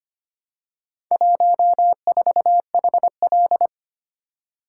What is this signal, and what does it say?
Morse code, heard as a keyed tone, about 25 wpm, 700 Hz. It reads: 14HL
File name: practice / 25wpm / Four Letter-Number Combinations - 25wpm